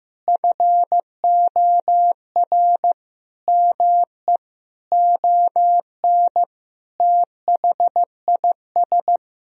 Morse code, heard as a keyed tone, about 15 wpm, 700 Hz. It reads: FOR ME ON THIS
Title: Sentences from Top 1000 Words - Review Part 2 - 15wpm